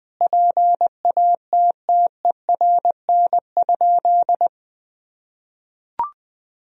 Morse code, heard as a keyed tone, about 20 wpm, 700 Hz. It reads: PATTERN?
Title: Sentences from Top 1000 Words - Review Part 2 - 20wpm